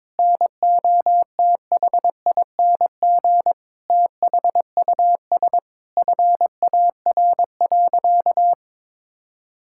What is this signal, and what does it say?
Morse code, heard as a keyed tone, about 22 wpm, 700 Hz. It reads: NOTHING THUS FAR.
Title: Sentences from Top 1000 Words - Review Part 5 - 22wpm